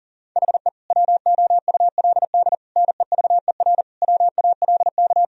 Morse code, heard as a keyed tone, about 39 wpm, 700 Hz. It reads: HE WOULD NEVER WALK
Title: Sentences from Top 1000 Words - Review Part 1 - 40wpm